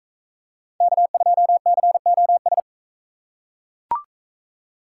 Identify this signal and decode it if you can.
Morse code, heard as a keyed tone, about 41 wpm, 700 Hz. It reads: K2COS